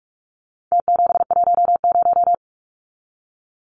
Morse code, heard as a keyed tone, about 44 wpm, 700 Hz. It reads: T710
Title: Numbers - Contest Style - 4-Digit 45wpm